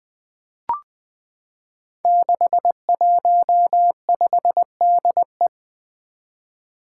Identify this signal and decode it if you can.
Morse code, heard as a keyed tone, about 20 wpm, 700 Hz. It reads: 615DE
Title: Five Challenging Letter-Number Combinations 20wpm